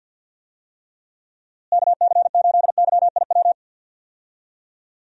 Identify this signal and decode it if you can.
Morse code, heard as a keyed tone, about 48 wpm, 700 Hz. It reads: KC8YIW
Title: USA Call Signs - 6 Characters - Repeated 3x - 50wpm